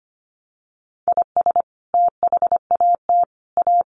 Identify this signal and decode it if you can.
Morse code, heard as a keyed tone, about 25 wpm, 700 Hz. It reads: IS THAT A
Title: Sentences from Top 1100 Words 25wpm